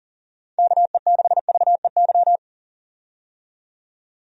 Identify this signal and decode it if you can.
Morse code, heard as a keyed tone, about 37 wpm, 700 Hz. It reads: KE6VEY
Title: USA Call Signs - 6 Characters - 40wpm